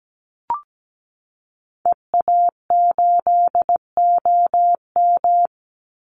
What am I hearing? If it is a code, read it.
Morse code, heard as a keyed tone, about 17 wpm, 700 Hz. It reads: EA8OM
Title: Call Signs Repeated 3x - 17wpm